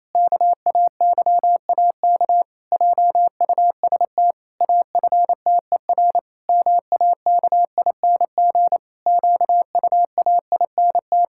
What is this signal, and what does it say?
Morse code, heard as a keyed tone, about 27 wpm, 700 Hz. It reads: KAYAK JUST AFTER MAKING QUAINT
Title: Pangram Challenge v2 - 28wpm